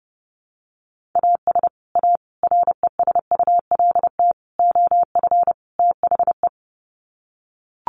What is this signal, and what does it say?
Morse code, heard as a keyed tone, about 30 wpm, 700 Hz. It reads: AS A RESULT OF THE
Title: Sets of 5 Words 30wpm